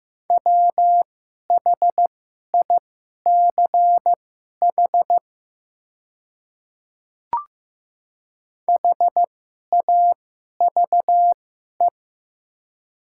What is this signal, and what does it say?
Morse code, heard as a keyed tone, about 15 wpm, spaced as for 11 wpm, 700 Hz. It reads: WHICH HAVE
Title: 2x Character Spacing - Top 300 Words - 15wpm